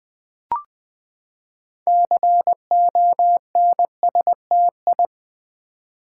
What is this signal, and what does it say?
Morse code, heard as a keyed tone, about 20 wpm, 700 Hz. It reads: CONSTI
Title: Build A Word - 6 Letters At A Time - 12-Letter Words - 20wpm